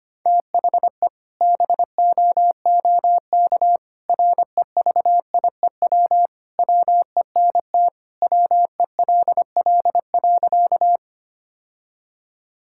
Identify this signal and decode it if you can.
Morse code, heard as a keyed tone, about 25 wpm, 700 Hz. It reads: THE BOOK REVIEW WENT WELL.
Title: Sentences from Top 1000 Words - Review Part 6 - 25wpm